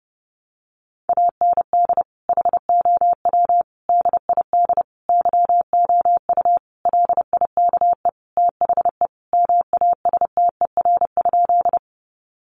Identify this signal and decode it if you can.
Morse code, heard as a keyed tone, about 28 wpm, 700 Hz. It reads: AND HOW DID YOU LIKE THE MASTER?
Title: Sentences from Top 1300 Words 30wpm